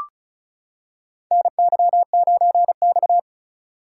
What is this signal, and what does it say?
Morse code, heard as a keyed tone, about 36 wpm, 700 Hz. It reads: NY9X
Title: K1USN SST QSO Practice - 35wpm